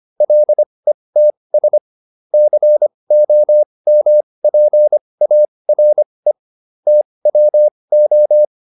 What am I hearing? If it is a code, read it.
Morse code, heard as a keyed tone, about 25 wpm, spaced as for 20 wpm, 600 Hz. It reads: LETS COMPARE TWO